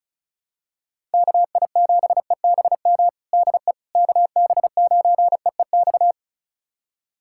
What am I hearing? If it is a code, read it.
Morse code, heard as a keyed tone, about 34 wpm, 700 Hz. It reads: KI7EBM DE KB9EEX
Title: USA Callsign DE Callsign - 6 Characters - 35wpm